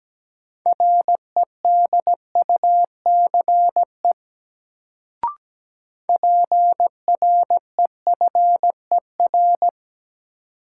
Morse code, heard as a keyed tone, about 17 wpm, 700 Hz. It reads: REDUCE PREFER
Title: Sentences from Top 1100 Words - New Word Drill - 17wpm